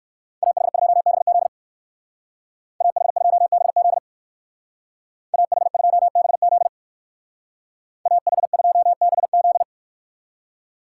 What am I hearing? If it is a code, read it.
Morse code, heard as a keyed tone, about 56 wpm, 700 Hz. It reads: AH2BZ AH2BZ AH2BZ AH2BZ
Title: Speed Racing - USA 2x2 Call Signs 45wpm